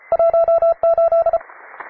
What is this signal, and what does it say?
Morse code, heard as a keyed tone, about 33 wpm, 670 Hz. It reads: 18
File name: wae / cw advanced / p3f cw